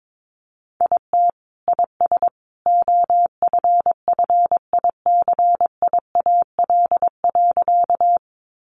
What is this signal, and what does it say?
Morse code, heard as a keyed tone, about 22 wpm, 700 Hz. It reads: IT IS OFFICIAL.
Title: Sentences from Top 1000 Words 22wpm